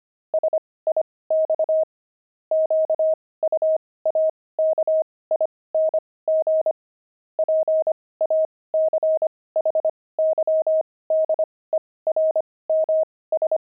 Morse code, heard as a keyed tone, about 25 wpm, spaced as for 18 wpm, 630 Hz. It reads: SIX QUAKING PACHYDERMS
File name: pangram-fs-18-cs-25-jelly